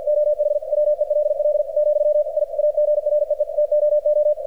Morse code, heard as a keyed tone, about 48 wpm, 600 Hz, with radio noise. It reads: OH WELL , I AGREE TOO
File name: archived / CRSnet